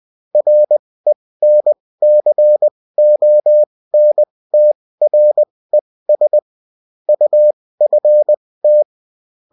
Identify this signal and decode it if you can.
Morse code, heard as a keyed tone, about 20 wpm, spaced as for 16 wpm, 600 Hz. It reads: RENCONTRES UFT
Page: bulletin-uft-sous-forme-texte-et-en-cw